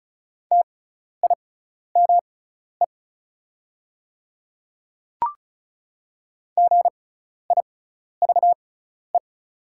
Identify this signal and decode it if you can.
Morse code, heard as a keyed tone, about 36 wpm, spaced as for 12 wpm, 700 Hz. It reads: TIME GIVE
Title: 6x Character Spacing - Top 500 Words - 35wpm